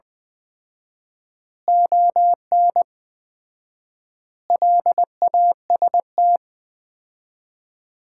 Morse code, heard as a keyed tone, about 20 wpm, 700 Hz. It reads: ON LAST
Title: Common QSO Phrases - 4x Word-Spacing - 20wpm